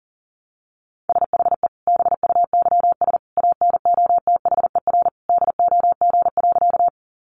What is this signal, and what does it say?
Morse code, heard as a keyed tone, about 40 wpm, 700 Hz. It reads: SHE BUYS ANOTHER DOG.
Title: Easy Phrases 40wpm